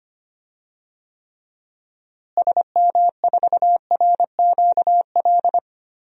Morse code, heard as a keyed tone, about 25 wpm, 700 Hz. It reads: SM4RQL
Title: DXCC-100 Call Signs - 6 Characters - Repeated 2x - 25wpm